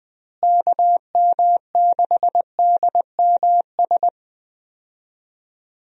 Morse code, heard as a keyed tone, about 21 wpm, 700 Hz. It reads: KM6DMS
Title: USA Call Signs - 6 Characters - Repeated 3x - 20wpm